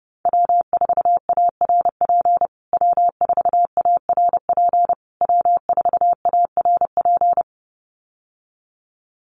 Morse code, heard as a keyed tone, about 30 wpm, 700 Hz. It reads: W4ARP W4ARP W4ARP
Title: USA Call Signs Repeated 3x 30wpm